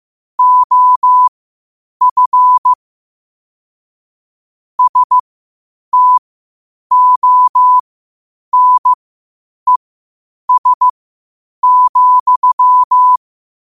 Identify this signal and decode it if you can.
Morse code, heard as a keyed tone, about 15 wpm, spaced as for 8 wpm, 1000 Hz. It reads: OF STONES,